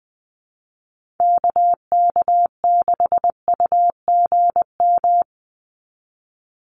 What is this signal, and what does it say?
Morse code, heard as a keyed tone, about 20 wpm, 700 Hz. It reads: KK6UGM
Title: USA Call Signs 20wpm